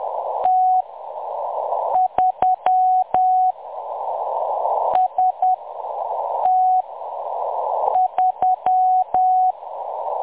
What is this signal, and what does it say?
Morse code, heard as a keyed tone, about 10 wpm, spaced as for 6 wpm, 740 Hz. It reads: T3 ST3
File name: M23-ST3- -0858utc-10184khz